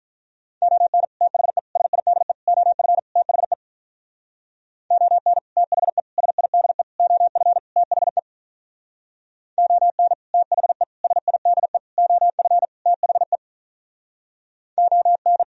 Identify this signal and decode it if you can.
Morse code, heard as a keyed tone, about 45 wpm, 700 Hz. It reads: ON THE SIDE OF THE ON THE SIDE OF THE ON THE SIDE OF THE ON